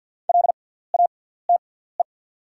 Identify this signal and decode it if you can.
Morse code, heard as a keyed tone, about 48 wpm, spaced as for 17 wpm, 700 Hz. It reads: LATE
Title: 6x Character Spacing - Top 300 Words - 50wpm